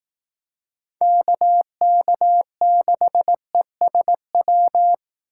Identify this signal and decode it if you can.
Morse code, heard as a keyed tone, about 18 wpm, 700 Hz. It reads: KK6ESW
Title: Speed Racing - USA 2x3 Call Signs 15wpm